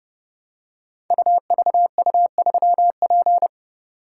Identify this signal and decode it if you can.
Morse code, heard as a keyed tone, about 28 wpm, 700 Hz. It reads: UVU3P
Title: Code Groups - 5 Letter:Numbers - 30wpm